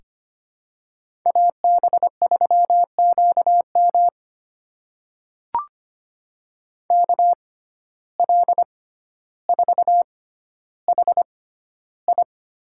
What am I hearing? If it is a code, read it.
Morse code, heard as a keyed tone, about 25 wpm, spaced as for 13 wpm, 700 Hz. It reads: AB3QM KL4HI
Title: USA Call Signs - 6x Character Spacing - 25wpm